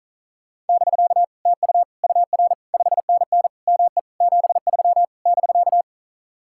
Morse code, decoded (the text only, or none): <BK> TU UR 5NN ME 73 <BK>